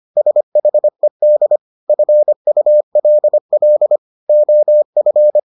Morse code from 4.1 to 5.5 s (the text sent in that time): OF